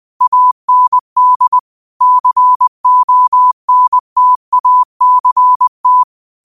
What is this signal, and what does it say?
Morse code, heard as a keyed tone, about 20 wpm, 1000 Hz. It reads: AND CONTACT
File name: qotdfiles / 20wpm